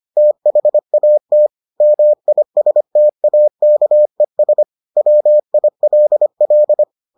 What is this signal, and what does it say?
Morse code, heard as a keyed tone, about 25 wpm, 600 Hz. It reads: THAT MISTAKES WILL